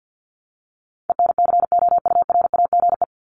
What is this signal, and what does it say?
Morse code, heard as a keyed tone, about 50 wpm, 700 Hz. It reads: ENCOURAGE
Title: Top 1251-1500 Words 50wpm